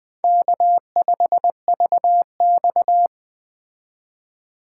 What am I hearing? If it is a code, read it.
Morse code, heard as a keyed tone, about 21 wpm, 700 Hz. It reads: K5VX
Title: Speed Racing - USA 1x2 Call Signs 17wpm